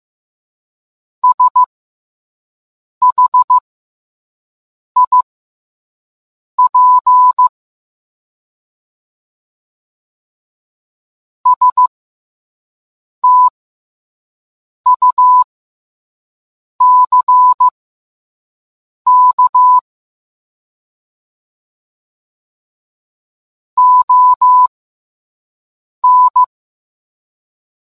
Morse code, heard as a keyed tone, about 15 wpm, spaced as for 5 wpm, 1000 Hz. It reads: SHIP STUCK ON